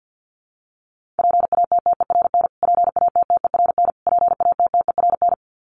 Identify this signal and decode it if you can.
Morse code, heard as a keyed tone, about 50 wpm, 700 Hz. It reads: PATTERN PATTERN PATTERN
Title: Top 500 Words Repeated 50wpm